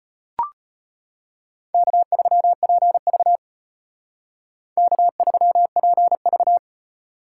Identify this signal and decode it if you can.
Morse code, heard as a keyed tone, about 32 wpm, 700 Hz. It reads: K3PV K3PV